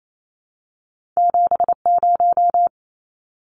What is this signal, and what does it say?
Morse code, heard as a keyed tone, about 28 wpm, 700 Hz. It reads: 70
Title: Numbers - 2-Digit 28wpm